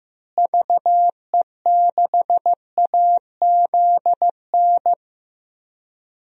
Morse code, heard as a keyed tone, about 15 wpm, 700 Hz. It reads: VE6AZN
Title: DXCC-100 Call Signs - 6 Characters - Repeated 3x - 15wpm